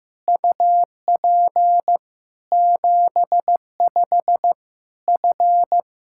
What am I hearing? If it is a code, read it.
Morse code, heard as a keyed tone, about 16 wpm, spaced as for 13 wpm, 700 Hz. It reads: UP 75F
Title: Practice QSO - Exchange 2 - Info - 15wpm